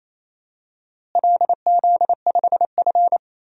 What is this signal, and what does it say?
Morse code, heard as a keyed tone, about 27 wpm, 700 Hz. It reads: LZ5F
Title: DXCC-100 Call Signs - 4 Characters - Repeated 3x - 28wpm